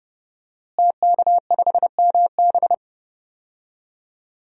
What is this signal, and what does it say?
Morse code, heard as a keyed tone, about 30 wpm, 700 Hz. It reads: TK5MB
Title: DXCC-100 Call Signs - 5 Characters - Repeated 3x - 30wpm